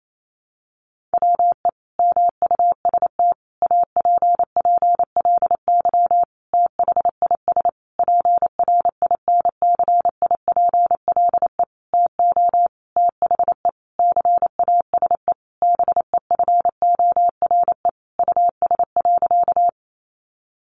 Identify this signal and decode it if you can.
Morse code, heard as a keyed tone, about 28 wpm, 700 Hz. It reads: WE MUST APPLY THIS PRINCIPLE TO THE CASE BEFORE US.
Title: Sentences from Top 1400 Words 28wpm